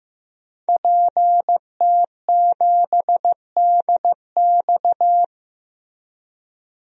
Morse code, heard as a keyed tone, about 15 wpm, 700 Hz. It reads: PT7DX
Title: DXCC-100 Call Signs - 5 Characters - Repeated 3x - 15wpm